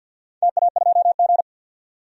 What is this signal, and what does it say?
Morse code, heard as a keyed tone, about 50 wpm, 700 Hz. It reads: TA2G
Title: DXCC-100 Call Signs - 4 Characters - 50wpm